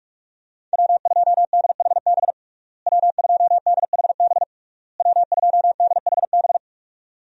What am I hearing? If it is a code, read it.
Morse code, heard as a keyed tone, about 41 wpm, 700 Hz. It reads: W2DHB W2DHB W2DHB
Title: USA Call Signs - 5 Characters - Repeated 3x - 45wpm